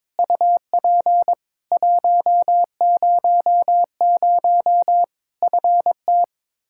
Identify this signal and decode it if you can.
Morse code, heard as a keyed tone, about 22 wpm, 700 Hz. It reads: UP 100 FT
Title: Practice QSO - Exchange 2 - Info - 22wpm